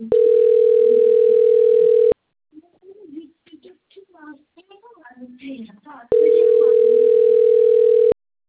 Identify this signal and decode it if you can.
Ringback tone (North America), 440 + 480 Hz, with speech.
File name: out- - - - - .57765